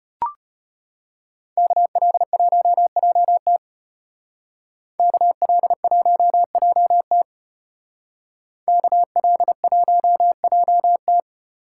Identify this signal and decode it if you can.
Morse code, heard as a keyed tone, about 34 wpm, 700 Hz. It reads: KL1JT KL1JT KL1JT